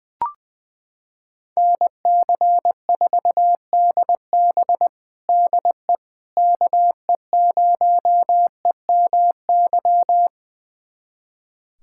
Morse code, heard as a keyed tone, about 20 wpm, 700 Hz. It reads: NC4DB DE KE0EMY
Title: USA Callsign DE Callsign 20wpm